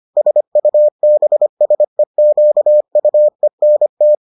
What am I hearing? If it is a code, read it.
Morse code, heard as a keyed tone, about 25 wpm, 600 Hz. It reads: SUBSEQUENT